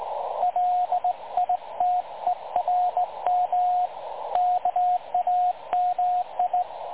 Morse code, heard as a keyed tone, about 19 wpm, 720 Hz, with radio noise. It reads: LITERM KAMI
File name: dlazaawansowanych / nagraniazpasma / websdr recording 3565.0kHz